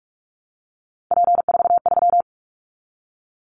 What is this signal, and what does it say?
Morse code, heard as a keyed tone, about 44 wpm, 700 Hz. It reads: P43
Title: Callsign Prefixes for Top 100 DXCC - 45wpm